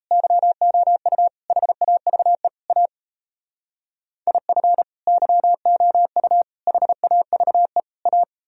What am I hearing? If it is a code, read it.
Morse code, heard as a keyed tone, about 34 wpm, 700 Hz. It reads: YOU HAVE A IF YOU HAVE A
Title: Speed-Racing - Sets of 4 Words 28wpm